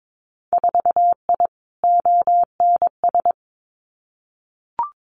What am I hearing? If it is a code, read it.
Morse code, heard as a keyed tone, about 22 wpm, 700 Hz. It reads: 4I ONS
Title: Winter Field Day Exchanges Encore 22wpm